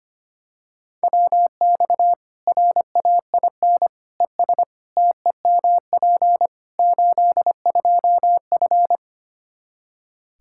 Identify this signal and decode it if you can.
Morse code, heard as a keyed tone, about 25 wpm, 700 Hz. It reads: WX RAIN ES TEMP 82F
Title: Practice QSO - Exchange 2 - Info - 25wpm